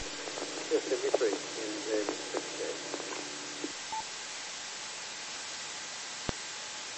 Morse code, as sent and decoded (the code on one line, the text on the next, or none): none